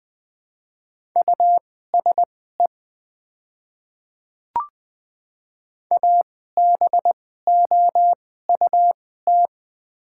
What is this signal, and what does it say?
Morse code, heard as a keyed tone, about 20 wpm, spaced as for 15 wpm, 700 Hz. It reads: USE ABOUT
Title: 2x Character Spacing - Top 100 Words - 20wpm